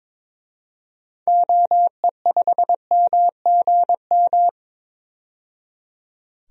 Morse code, heard as a keyed tone, about 22 wpm, 700 Hz. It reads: OE5MGM
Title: DXCC-100 Call Signs - 6 Characters - 22wpm